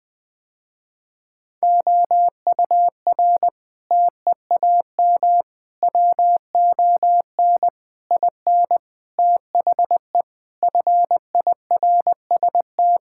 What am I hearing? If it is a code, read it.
Morse code, heard as a keyed tone, about 20 wpm, 700 Hz. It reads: OUR TEAM WON IN THE FIRST
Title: Sentences with 7-Word Phrases - Spoken Beginning - 20wpm